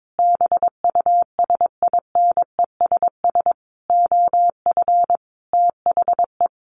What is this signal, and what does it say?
Morse code, heard as a keyed tone, about 22 wpm, 700 Hz. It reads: BUSINESS OF THE
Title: Sentences from Top 1500 Words 22wpm